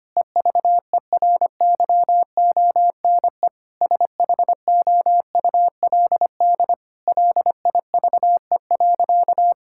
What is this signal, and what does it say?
Morse code, heard as a keyed tone, about 25 wpm, 700 Hz. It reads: EVERYONE SHOULD LIVE.
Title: Sentences from Top 1000 Words - Review Part 5 - 25wpm